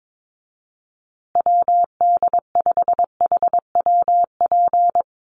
Morse code, heard as a keyed tone, about 22 wpm, 700 Hz. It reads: WD5HWP
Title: USA Call Signs Repeated 2x 22wpm